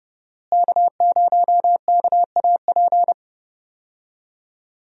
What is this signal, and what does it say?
Morse code, heard as a keyed tone, about 28 wpm, 700 Hz. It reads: K0KAP
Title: USA Call Signs - 5 Characters - Repeated 3x - 30wpm